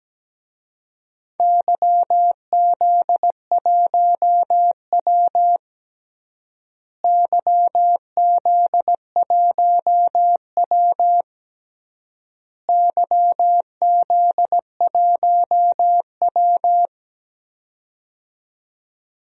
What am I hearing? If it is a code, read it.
Morse code, heard as a keyed tone, about 17 wpm, 700 Hz. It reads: YZ1W YZ1W YZ1W
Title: DXCC-100 Call Signs - 4 Characters - Repeated 3x - 17wpm